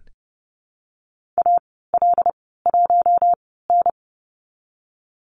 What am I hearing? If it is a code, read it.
Morse code, heard as a keyed tone, about 30 wpm, spaced as for 17 wpm, 700 Hz. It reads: AL1N
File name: Lesson 14g - Number 2 - Call Signs 30wpm